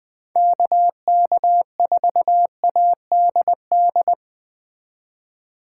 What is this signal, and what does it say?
Morse code, heard as a keyed tone, about 20 wpm, 700 Hz. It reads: KK4ADD
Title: Speed Racing - USA 2x3 Call Signs 17wpm